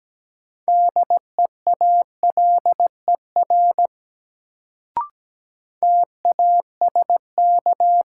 Morse code, heard as a keyed tone, about 17 wpm, 700 Hz. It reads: DEALER TASK
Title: Sentences from Top 1100 Words - New Word Drill Alt - 17wpm